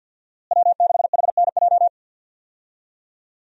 Morse code, heard as a keyed tone, about 50 wpm, 700 Hz. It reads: W6HNJ
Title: USA Call Signs - 5 Characters - Repeated 3x - 50wpm